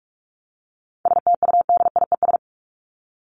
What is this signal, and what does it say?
Morse code, heard as a keyed tone, about 44 wpm, 700 Hz. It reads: STUDIES
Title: ES:S Suffix - Top 100 Words with Base Words 45wpm